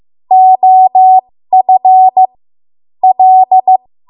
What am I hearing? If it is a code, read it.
Morse code, heard as a keyed tone, about 15 wpm, spaced as for 13 wpm, 750 Hz. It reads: OF L
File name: Morse / 13WPM